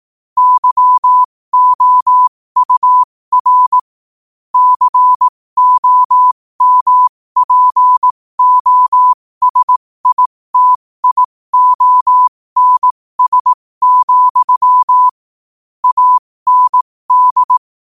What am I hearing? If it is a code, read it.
Morse code, heard as a keyed tone, about 18 wpm, spaced as for 15 wpm, 1000 Hz. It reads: YOUR COMPOSITIONS, AND
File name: qotdfiles / 15wpm